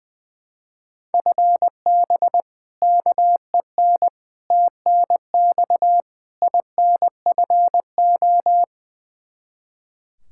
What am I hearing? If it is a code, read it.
Morse code, heard as a keyed tone, about 21 wpm, 700 Hz. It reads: FB KEN TNX INFO
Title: Practice QSO - Exchange 3 - More Info - 20wpm